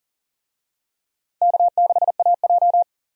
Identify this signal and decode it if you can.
Morse code, heard as a keyed tone, about 37 wpm, 700 Hz. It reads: K6AJ
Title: USA Call Signs - 4 Characters - 40wpm